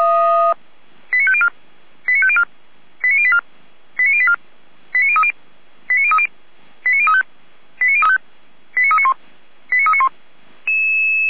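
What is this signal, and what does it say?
Five-tone selective call (ZVEI), heard as quick runs of five tones, tones of about 70 ms. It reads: 89504 89504 89084 89084 89030 89030 89035 89391 89391
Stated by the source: Received by SDR.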